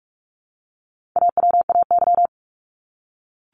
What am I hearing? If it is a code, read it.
Morse code, heard as a keyed tone, about 44 wpm, 700 Hz. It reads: AWAY
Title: Top 501-750 Words 45wpm